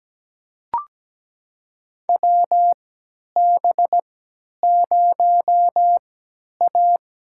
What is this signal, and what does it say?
Morse code, heard as a keyed tone, about 17 wpm, spaced as for 10 wpm, 700 Hz. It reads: WB0A
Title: USA Call Signs - 3x Character Spacing - 17wpm